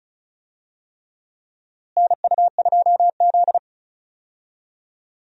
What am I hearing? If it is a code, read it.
Morse code, heard as a keyed tone, about 36 wpm, 700 Hz. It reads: NU2Z
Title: USA Call Signs - 4 Characters - Repeated 3x - 35wpm